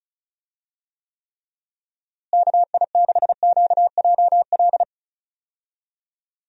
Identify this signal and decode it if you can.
Morse code, heard as a keyed tone, about 36 wpm, 700 Hz. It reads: KI6QJL
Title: USA Call Signs - 6 Characters - Repeated 2x - 35wpm